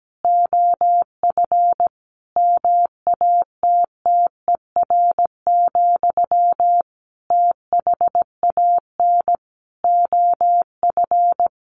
Morse code, heard as a keyed tone, about 17 wpm, 700 Hz. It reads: OF MATTER, THAN OF
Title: Sentences from Top 1400 Words 17wpm